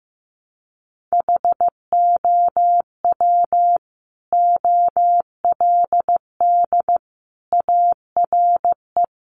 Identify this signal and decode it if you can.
Morse code, heard as a keyed tone, about 15 wpm, 700 Hz. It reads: HOW OLD ARE
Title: Sentences from Top 200 Words 15wpm